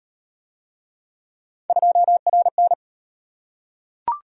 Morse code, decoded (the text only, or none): 2RN